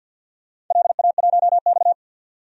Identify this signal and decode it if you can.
Morse code, heard as a keyed tone, about 51 wpm, 700 Hz. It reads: LA1X
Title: DXCC-100 Call Signs - 4 Characters - 50wpm